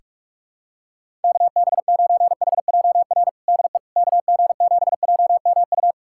Morse code, heard as a keyed tone, about 46 wpm, 700 Hz. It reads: KB9HJR DE KG7JMU
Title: USA Callsign DE Callsign - 6 Characters - 45wpm